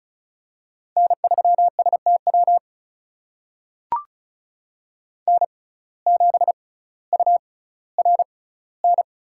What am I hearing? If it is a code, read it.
Morse code, heard as a keyed tone, about 36 wpm, spaced as for 18 wpm, 700 Hz. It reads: N3STW N7URN
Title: USA Call Signs - 6x Character Spacing - 35wpm